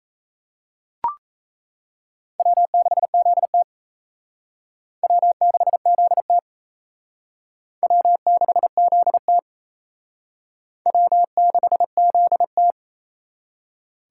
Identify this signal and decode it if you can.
Morse code, heard as a keyed tone, about 34 wpm, 700 Hz. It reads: W6ZT W6ZT W6ZT W6ZT
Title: Speed Racing - USA 1x2 Call Signs 28wpm